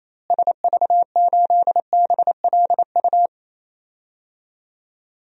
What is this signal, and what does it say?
Morse code, heard as a keyed tone, about 27 wpm, 700 Hz. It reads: SV8BLU
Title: DXCC-100 Call Signs - 6 Characters - Repeated 2x - 28wpm